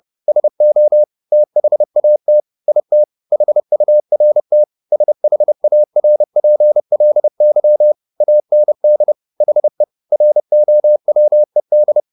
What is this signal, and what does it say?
Morse code, heard as a keyed tone, about 30 wpm, 600 Hz. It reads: SO THAT IT HURT SHARPLY AND HE ROWED
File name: the old man and the sea / the old man and the sea 1 30wpm